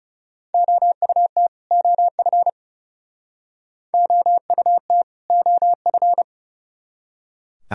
Morse code, heard as a keyed tone, about 36 wpm, 700 Hz. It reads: OUT OF OUT OF
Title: Speed-Racing - Sets of 2 Words 30wpm